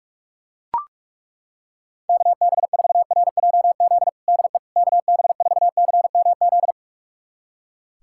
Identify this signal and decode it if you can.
Morse code, heard as a keyed tone, about 41 wpm, 700 Hz. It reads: KB4RJZ DE KB4CMZ
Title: USA Callsign DE Callsign - 6 Characters - 45wpm